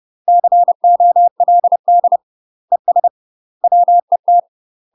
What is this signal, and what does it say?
Morse code, heard as a keyed tone, about 30 wpm, 700 Hz. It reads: COLD ES WET
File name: UK MASTER QSOs TRAINING / 30WPM / UK MASTER QSOs 0030